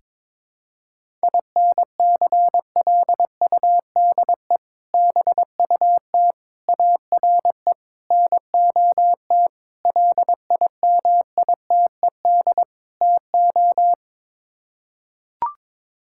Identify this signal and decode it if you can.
Morse code, heard as a keyed tone, about 22 wpm, 700 Hz. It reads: INCLUDE BUT ARE NOT LIMITED TO
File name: practice / 22wpm / Sets of 6 Words - 22wpm